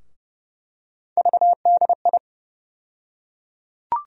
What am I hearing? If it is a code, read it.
Morse code, heard as a keyed tone, about 30 wpm, 700 Hz. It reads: VDI
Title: USA Call Sign Postfixes - 3 Letters 30wpm